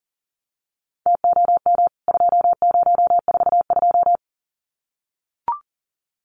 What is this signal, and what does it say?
Morse code, heard as a keyed tone, about 40 wpm, 700 Hz. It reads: TOM 2042
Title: CWT Exchange 40wpm